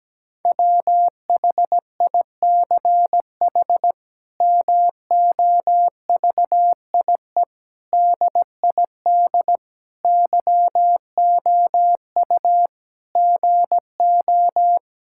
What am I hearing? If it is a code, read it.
Morse code, heard as a keyed tone, about 17 wpm, 700 Hz. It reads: WHICH MOVIE DID YOU GO
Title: Sentences from Top 1000 Words - Review Part 5 - 17wpm